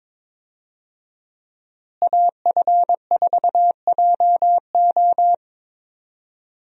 Morse code, heard as a keyed tone, about 22 wpm, 700 Hz. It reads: AF4JO
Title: USA Call Signs - 5 Characters - Repeated 3x - 22wpm